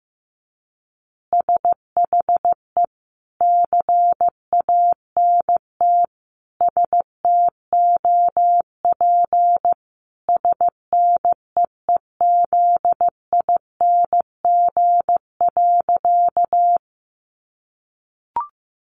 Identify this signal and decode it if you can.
Morse code, heard as a keyed tone, about 15 wpm, 700 Hz. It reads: SHE CANT STOP SNEEZING.
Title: Easy Phrases 15wpm